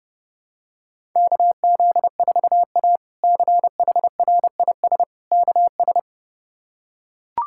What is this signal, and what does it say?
Morse code, heard as a keyed tone, about 30 wpm, 700 Hz. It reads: KZ4A CHRIS KS
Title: K1USN SST QSO Practice - 30wpm